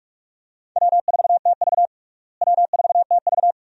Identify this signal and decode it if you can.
Morse code, heard as a keyed tone, about 41 wpm, 700 Hz. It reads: W4TV W4TV
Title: USA Call Signs - 4 Characters - Repeated 2x - 45wpm